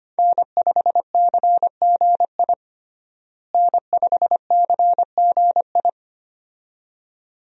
Morse code, heard as a keyed tone, about 25 wpm, 700 Hz. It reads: N5CGI N5CGI
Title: USA Call Signs - 5 Characters - Repeated 2x - 25wpm